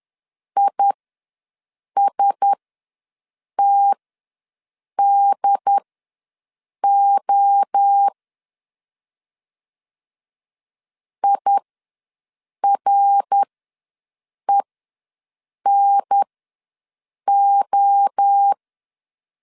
Morse code, heard as a keyed tone, about 10 wpm, spaced as for 6 wpm, 800 Hz. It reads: ISTDO IRENO
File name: LEZIONI / 5 LBVC / 5 LBV ws 10wpm 3645